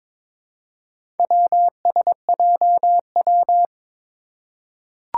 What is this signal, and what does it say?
Morse code, heard as a keyed tone, about 22 wpm, 700 Hz. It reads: WSJW
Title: Four Challenging Letter-Number Combinations 22wpm